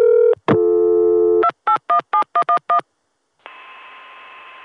DTMF, heard as telephone touch tones, tones of about 100 ms. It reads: 2817111